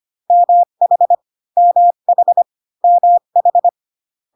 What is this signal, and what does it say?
Morse code, heard as a keyed tone, about 25 wpm, 700 Hz. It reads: MH MH MH